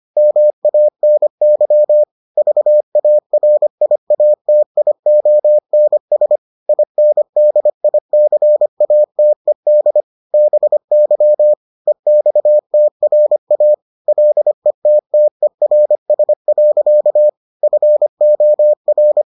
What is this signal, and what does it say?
Morse code, heard as a keyed tone, about 25 wpm, 600 Hz. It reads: MANY VARIATIONS INDICATED BY EXTRA LETTERS. FOR